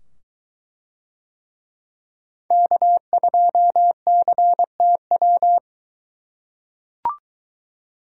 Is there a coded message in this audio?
Yes: Morse code from 2.5 to 5.6 s, heard as a keyed tone, about 21 wpm, 700 Hz. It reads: K2CTW